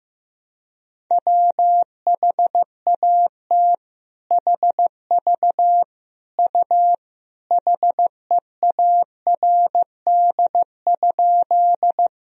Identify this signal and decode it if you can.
Morse code, heard as a keyed tone, about 16 wpm, 700 Hz. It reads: WHAT HV U HEARD?
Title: Sentences from Top 400 Words and Abbreviations 15wpm